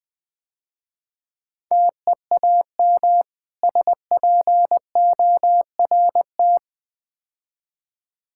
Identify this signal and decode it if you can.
Morse code, heard as a keyed tone, about 23 wpm, 700 Hz. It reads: TEAM SPORT
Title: Difficult Words - Speed-Racing with Hints - 20wpm